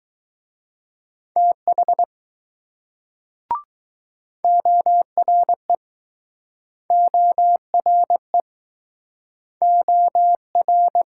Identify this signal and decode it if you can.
Morse code, heard as a keyed tone, about 21 wpm, 700 Hz. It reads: TH ORE ORE OR